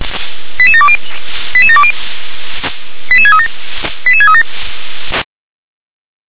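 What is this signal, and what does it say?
Five-tone selective call (ZVEI), heard as quick runs of five tones, tones of about 70 ms. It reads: 88720 88720 88637 88637
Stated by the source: ZVEI five-tone selective call received on SDR.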